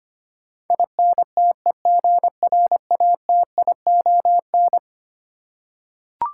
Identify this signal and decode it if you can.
Morse code, heard as a keyed tone, about 25 wpm, 700 Hz. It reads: INTEGRATION
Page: morse-code-ninja-practice-25wpm